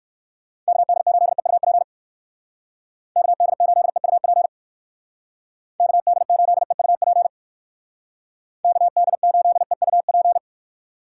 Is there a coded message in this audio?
Yes: Morse code from 0.7 to 10.4 s, heard as a keyed tone, about 59 wpm, 700 Hz. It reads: KD8EUP KD8EUP KD8EUP KD8EUP